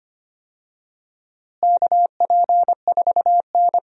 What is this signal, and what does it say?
Morse code, heard as a keyed tone, about 25 wpm, 700 Hz. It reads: KP4N
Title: DXCC-100 Call Signs - 4 Characters - Repeated 3x - 25wpm